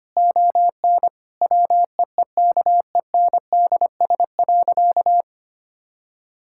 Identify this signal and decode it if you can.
Morse code, heard as a keyed tone, about 25 wpm, 700 Hz. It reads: ON WEEKENDS.
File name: practice / 25wpm / Difficult Sentences with Hints - 25wpm